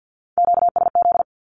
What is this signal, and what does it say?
Morse code, heard as a keyed tone, about 50 wpm, 700 Hz. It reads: QSZ
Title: Q-Codes with Spoken Meaning - 50wpm